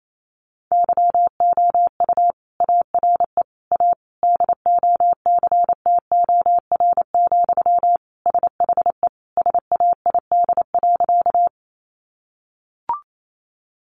Morse code, heard as a keyed tone, about 28 wpm, 700 Hz. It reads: YOU ARE A DOCTOR, SHE SAID.
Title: Sentences from Top 1500 Words 28wpm